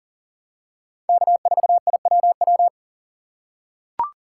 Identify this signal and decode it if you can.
Morse code, heard as a keyed tone, about 38 wpm, 700 Hz. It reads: K4IWW
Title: USA Call Signs - 5 Characters - 2x Character Spacing - 40wpm